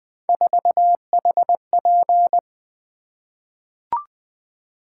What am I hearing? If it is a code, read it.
Morse code, heard as a keyed tone, about 20 wpm, 700 Hz. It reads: 4HP
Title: Three Letter-Number Combinations - 20wpm